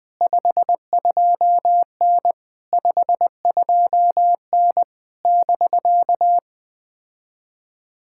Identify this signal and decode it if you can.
Morse code, heard as a keyed tone, about 20 wpm, 700 Hz. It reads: 52N 52N <BK>
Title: POTA QSO Practice - 20wpm